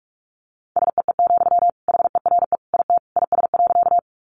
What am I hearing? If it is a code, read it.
Morse code, heard as a keyed tone, about 44 wpm, 700 Hz. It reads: SEE, HERE IT IS.
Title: Sentences from Top 200 Words 45wpm